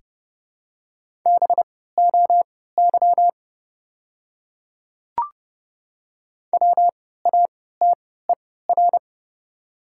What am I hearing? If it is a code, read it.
Morse code, heard as a keyed tone, about 30 wpm, spaced as for 17 wpm, 700 Hz. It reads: BOY WATER
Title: 3x Character Spacing - Top 200 Words - 30wpm